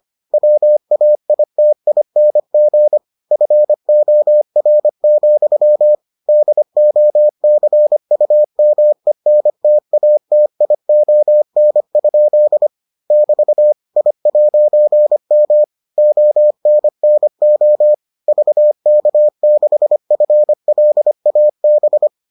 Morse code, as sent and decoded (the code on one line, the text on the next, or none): .-- .- .. - .. -. --. / ..-. --- .-. --..-- / -.. --- -.-. ..- -- . -. - .- - .. --- -. ..--.. / -...- / .. .----. -- / --- -. -. --- / ...- -.- -.... ..-. .-.. .- -...
WAITING FOR, DOCUMENTATION? <BT> I'M ONNO VK6FLAB